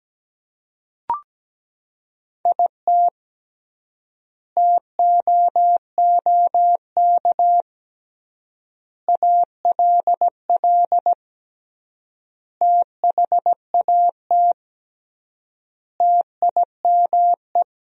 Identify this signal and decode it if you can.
Morse code, heard as a keyed tone, about 17 wpm, 700 Hz. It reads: IT TOOK ALL THAT TIME
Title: 3x Word-Spacing - Sentences from Top 300 Words - 17wpm